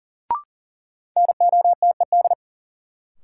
Morse code, heard as a keyed tone, about 40 wpm, 700 Hz. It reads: NOTED
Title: Sets of 5 Words - New Word Drill - 40wpm